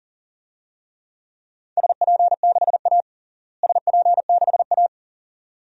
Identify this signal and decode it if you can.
Morse code, heard as a keyed tone, about 40 wpm, 700 Hz. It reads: SP6A SP6A
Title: DXCC-100 Call Signs - 4 Characters - Repeated 2x - 40wpm